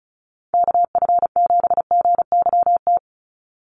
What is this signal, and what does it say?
Morse code, heard as a keyed tone, about 34 wpm, 700 Hz. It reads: KF7GYT
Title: USA Call Signs 35wpm